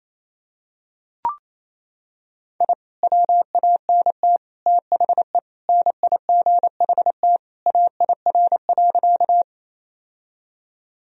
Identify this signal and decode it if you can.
Morse code, heard as a keyed tone, about 28 wpm, 700 Hz. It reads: I WANT THE NIGHT AIR.
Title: Sentences from Top 1000 Words - Review Part 1 - 28wpm